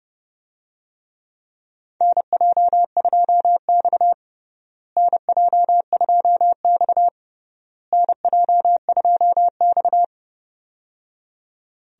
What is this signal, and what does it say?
Morse code, heard as a keyed tone, about 30 wpm, 700 Hz. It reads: NJ2X NJ2X NJ2X
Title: USA Call Signs - 4 Characters - Repeated 3x - 30wpm